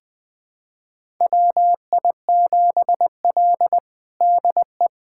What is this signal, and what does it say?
Morse code, heard as a keyed tone, about 21 wpm, 700 Hz. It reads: WI7L DE
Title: USA Callsign DE Callsign - 4 Characters - 20wpm